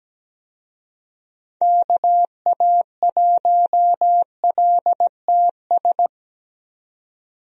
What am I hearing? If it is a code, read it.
Morse code, heard as a keyed tone, about 17 wpm, 700 Hz. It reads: KA1LTS
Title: USA Call Signs - 6 Characters - 17wpm